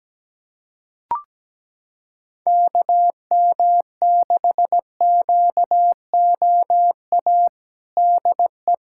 Morse code, heard as a keyed tone, about 17 wpm, 700 Hz. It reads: KM6QOA DE
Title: USA Callsign DE Callsign - 6 Characters - 17wpm